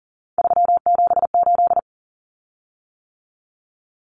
Morse code, heard as a keyed tone, about 40 wpm, 700 Hz. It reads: 378
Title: Common Cut Numbers - 3-Digit 40wpm